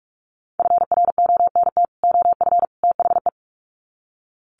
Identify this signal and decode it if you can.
Morse code, heard as a keyed tone, about 44 wpm, 700 Hz. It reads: FRONT OF THE
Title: Sets of 3 Words 45wpm